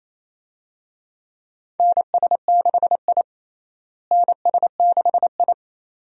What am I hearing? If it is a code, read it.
Morse code, heard as a keyed tone, about 27 wpm, 700 Hz. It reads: NS6I NS6I
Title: USA Call Signs - 4 Characters - Repeated 2x - 28wpm